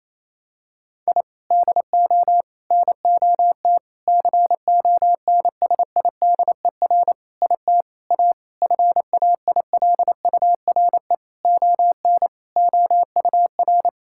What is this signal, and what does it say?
Morse code, heard as a keyed tone, about 28 wpm, 700 Hz. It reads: I DO NOT CONSIDER IT A FAILURE ON OUR
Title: Sentences from Top 1000 Words - Review Part 6 - 28wpm